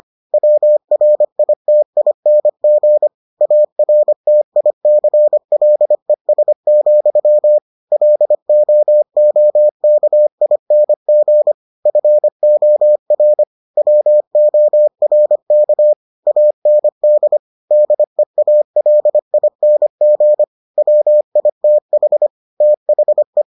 Morse code, heard as a keyed tone, about 25 wpm, 600 Hz. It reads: WRITING ARTICLES, LOOKING FOR WORK AND DEALING WITH THE